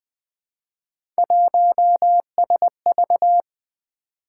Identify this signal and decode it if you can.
Morse code, heard as a keyed tone, about 20 wpm, 700 Hz. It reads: 1SV
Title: Three Letter-Number Combinations - 20wpm